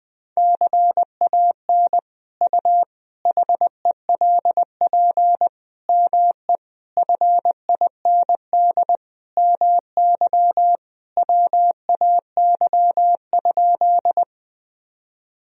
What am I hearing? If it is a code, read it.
Morse code, heard as a keyed tone, about 20 wpm, 700 Hz. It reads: CAN U HELP ME FIND MY WAY?
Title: Sentences from Top 200 Words and Abbreviations 20wpm